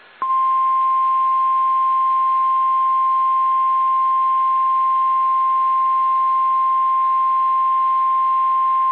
Emergency Alert System (SAME) tones (the 1050 Hz attention tone). It is heard from 0.6 to 8.6 s.
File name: eas / KSDZKDJL / KSDZKDJL